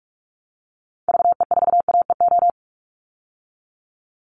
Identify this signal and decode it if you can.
Morse code, heard as a keyed tone, about 44 wpm, 700 Hz. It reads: VE4AEO
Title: Call Signs Repeated 2x - 45wpm